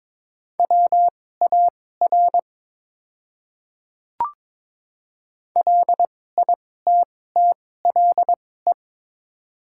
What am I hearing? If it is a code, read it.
Morse code, heard as a keyed tone, about 22 wpm, spaced as for 16 wpm, 690 Hz. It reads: WAR LITTLE
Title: 2x Character Spacing - Top 500 Words - 22wpm